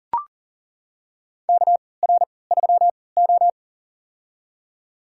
Morse code, heard as a keyed tone, about 40 wpm, spaced as for 23 wpm, 700 Hz. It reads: KR3O
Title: USA Call Signs - 4 Characters - 3x Character Spacing - 40wpm